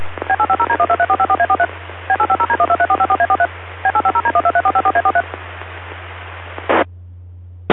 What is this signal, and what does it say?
DTMF, heard as telephone touch tones, tones of about 70 ms. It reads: B850D123454A43 B850D123454A43 B850D123454A43
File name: hamrepeaters / zavar / HG9RVC